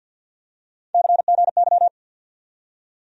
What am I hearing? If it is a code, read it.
Morse code, heard as a keyed tone, about 47 wpm, 700 Hz. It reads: CGY
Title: USA Call Sign Postfixes - 3 Letters 50wpm